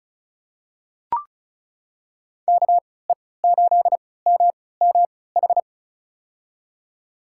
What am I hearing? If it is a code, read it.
Morse code, heard as a keyed tone, about 34 wpm, spaced as for 20 wpm, 700 Hz. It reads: KE8MMH
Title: USA Call Signs - 6 Characters - 3x Character Spacing - 35wpm